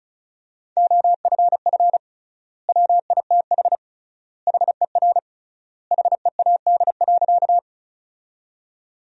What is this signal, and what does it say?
Morse code, heard as a keyed tone, about 36 wpm, 700 Hz. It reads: OFF WITH HER HEAD.
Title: 3x Word-Spacing - Sentences from Top 300 Words - 35wpm